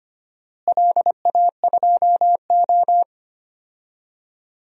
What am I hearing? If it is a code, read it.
Morse code, heard as a keyed tone, about 25 wpm, 700 Hz. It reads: LA2O
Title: DXCC-100 Call Signs - 4 Characters - Repeated 3x - 25wpm